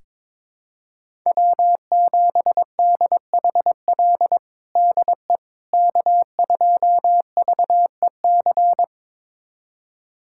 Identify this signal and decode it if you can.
Morse code, heard as a keyed tone, about 22 wpm, 700 Hz. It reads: W7DHL DE K2VEC
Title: USA Callsign DE Callsign - 5 Characters - 22wpm